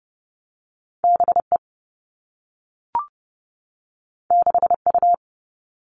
Morse code, heard as a keyed tone, about 30 wpm, 700 Hz. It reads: BE 6U
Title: Two Challenging Letter-Number Combinations 30wpm